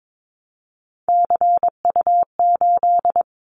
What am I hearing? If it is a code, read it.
Morse code, heard as a keyed tone, about 22 wpm, 700 Hz. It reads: CU8
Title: Callsign Prefixes for Top 50 DXCC - 22wpm